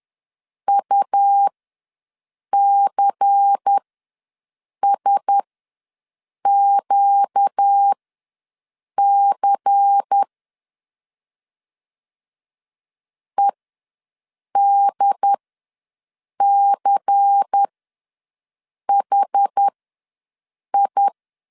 Morse code, heard as a keyed tone, about 10 wpm, spaced as for 6 wpm, 800 Hz. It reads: UCSQC EDCHI
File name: LEZIONI / 5 LBVC / 5 LBVC ws 10wpm 3733